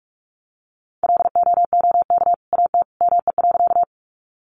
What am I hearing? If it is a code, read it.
Morse code, heard as a keyed tone, about 44 wpm, 700 Hz. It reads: LOOK AT ME.
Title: Sentences from Top 200 Words 45wpm